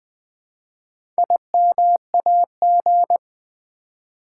Morse code, heard as a keyed tone, about 20 wpm, 700 Hz. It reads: IMAG
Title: Build A Word - 1 Letter At A Time - 5-Letter Words - 20wpm